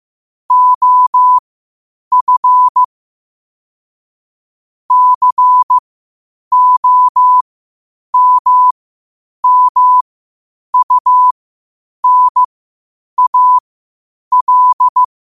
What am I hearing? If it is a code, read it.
Morse code, heard as a keyed tone, about 15 wpm, spaced as for 8 wpm, 1000 Hz. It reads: OF COMMUNAL
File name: qotdfiles / 8wpm